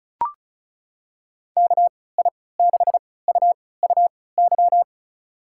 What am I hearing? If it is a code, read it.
Morse code, heard as a keyed tone, about 35 wpm, spaced as for 20 wpm, 700 Hz. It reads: KI6UUY
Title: USA Call Signs - 6 Characters - 3x Character Spacing - 35wpm